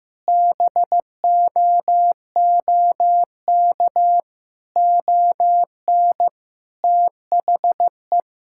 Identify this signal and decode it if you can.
Morse code, heard as a keyed tone, about 16 wpm, 700 Hz. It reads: BOOK ON THE
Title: Sentences from Top 1000 Words - Review Part 6 - 15wpm